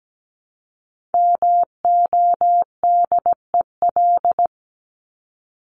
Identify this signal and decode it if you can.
Morse code, heard as a keyed tone, about 17 wpm, 700 Hz. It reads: MODEL
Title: Top 501-750 Words 17wpm